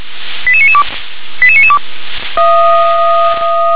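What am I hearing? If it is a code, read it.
Five-tone selective call (ZVEI), heard as quick runs of five tones, tones of about 70 ms. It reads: 88992 88992
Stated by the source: Five-tone ZVEI selective call received on SDR.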